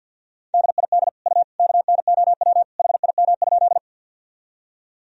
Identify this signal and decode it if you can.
Morse code, heard as a keyed tone, about 50 wpm, 700 Hz. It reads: DID U KNOW HIM?
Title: Sentences from Top 100 Words and Abbreviations 50wpm